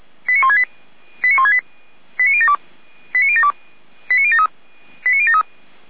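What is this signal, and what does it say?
Five-tone selective call (ZVEI), heard as quick runs of five tones, tones of about 70 ms. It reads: 89168 89168 89072 89072 89073 89073
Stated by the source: ZVEI five-tone selective call 89150, received via SDR.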